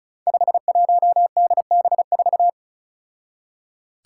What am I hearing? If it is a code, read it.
Morse code, heard as a keyed tone, about 36 wpm, 700 Hz. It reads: 51DB4
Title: Five Challenging Letter-Number Combinations 35wpm